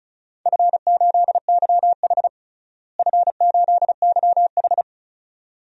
Morse code, heard as a keyed tone, about 36 wpm, 700 Hz. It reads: F8YH F8YH
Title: DXCC-100 Call Signs - 4 Characters - Repeated 2x - 35wpm